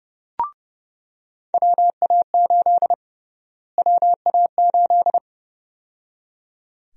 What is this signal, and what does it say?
Morse code, heard as a keyed tone, about 30 wpm, 700 Hz. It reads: WA8 WA8
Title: USA Call Sign Prefixes - Repeated 2x - 30wpm